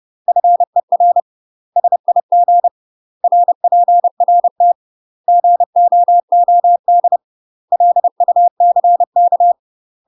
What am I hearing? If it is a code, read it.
Morse code, heard as a keyed tone, about 30 wpm, 700 Hz. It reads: FER SIG RPRT GOOD LUCK
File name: UK MASTER QSOs TRAINING / 30WPM / UK MASTER QSOs 0036